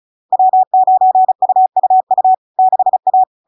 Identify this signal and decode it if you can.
Morse code, heard as a keyed tone, about 36 wpm, 750 Hz. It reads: W9UUU 6A
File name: LARK-FD-2022-Log-in-CW-35wpm-750-Hz